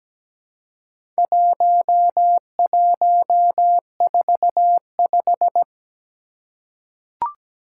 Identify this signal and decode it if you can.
Morse code, heard as a keyed tone, about 17 wpm, 700 Hz. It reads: 1145
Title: Four Challenging Letter-Number Combinations 17wpm